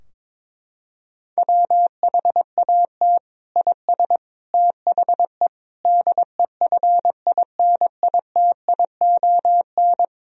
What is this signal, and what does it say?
Morse code, heard as a keyed tone, about 22 wpm, 700 Hz. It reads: WHAT IS THE DEFINITION
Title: Sentences from Top 1000 Words - Review Part 4 - 22wpm